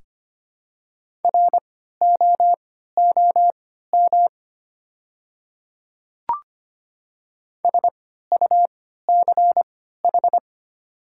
Morse code, heard as a keyed tone, about 25 wpm, spaced as for 14 wpm, 700 Hz. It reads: ROOM SUCH
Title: 3x Character Spacing - Top 300 Words - 25wpm